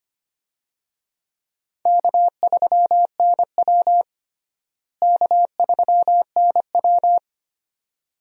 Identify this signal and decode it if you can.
Morse code, heard as a keyed tone, about 25 wpm, 700 Hz. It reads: K3NW K3NW
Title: USA Call Signs - 4 Characters - Repeated 2x - 25wpm